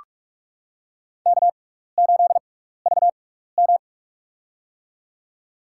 Morse code, heard as a keyed tone, about 43 wpm, spaced as for 15 wpm, 700 Hz. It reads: K8VM
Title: USA Call Signs - 4 Characters - 6x Character Spacing - 45wpm